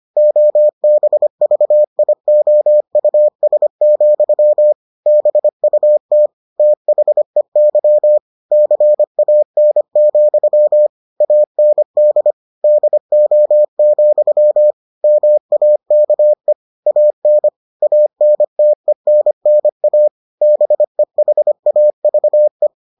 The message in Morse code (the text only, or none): OBVIOUS, BUT THEY CAN, AND DO, MAKE AN ANTENNA BEHAVE